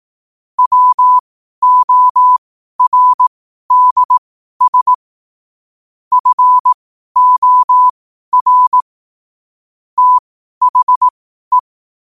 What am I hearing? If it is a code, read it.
Morse code, heard as a keyed tone, about 18 wpm, spaced as for 12 wpm, 1000 Hz. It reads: WORDS FOR THE
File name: qotdfiles / 12wpm